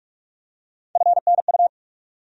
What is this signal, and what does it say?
Morse code, heard as a keyed tone, about 47 wpm, 700 Hz. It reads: FNU
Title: USA Call Sign Postfixes - 3 Letters 45wpm